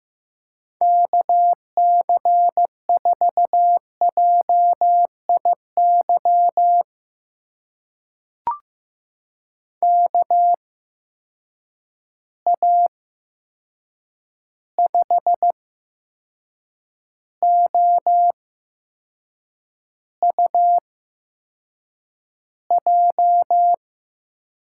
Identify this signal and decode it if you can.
Morse code, heard as a keyed tone, about 15 wpm, spaced as for 6 wpm, 700 Hz. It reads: KC4JIY KA5OUJ
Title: USA Call Signs - 6 Characters - 8x Character Spacing - 15wpm